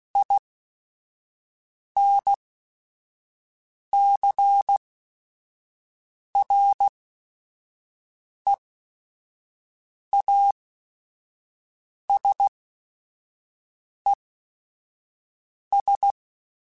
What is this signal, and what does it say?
Morse code, heard as a keyed tone, about 16 wpm, spaced as for 5 wpm, 800 Hz. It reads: INCREASES
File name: Technical Subjects / Technical Subjects-mp3s / XL @5